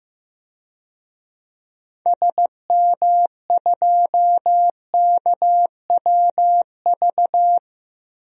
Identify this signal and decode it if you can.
Morse code, heard as a keyed tone, about 15 wpm, 700 Hz. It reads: SM2KWV
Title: DXCC-100 Call Signs - 6 Characters - Repeated 2x - 15wpm